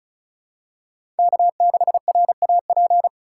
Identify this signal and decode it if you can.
Morse code, heard as a keyed tone, about 34 wpm, 700 Hz. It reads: K6RAP
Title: USA Call Signs - 4x Character Spacing - 35wpm